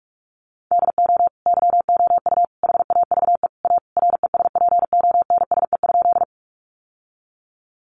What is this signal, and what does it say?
Morse code, heard as a keyed tone, about 44 wpm, 700 Hz. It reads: DO YOU HAVE A RESPONSE?
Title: Sentences from Top 1400 Words 45wpm